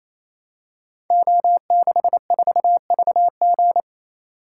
Morse code, heard as a keyed tone, about 29 wpm, 700 Hz. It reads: O64VG
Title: Code Groups - 5 Letter:Numbers - 28wpm